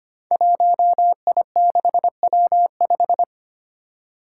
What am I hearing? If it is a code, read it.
Morse code, heard as a keyed tone, about 25 wpm, 700 Hz. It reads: 1I6W5
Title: Five Challenging Letter-Number Combinations 25wpm